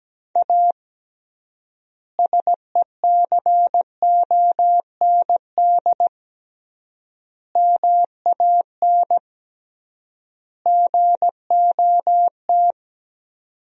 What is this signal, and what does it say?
Morse code, heard as a keyed tone, about 17 wpm, 700 Hz. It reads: A SECOND MAN GOT
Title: 3x Word-Spacing - Sentences from Top 300 Words - 17wpm